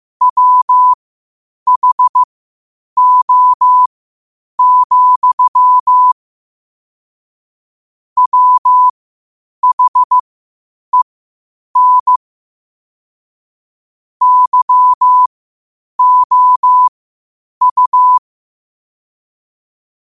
Morse code, heard as a keyed tone, about 15 wpm, spaced as for 8 wpm, 1000 Hz. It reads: WHO, WHEN YOU
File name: 8wpm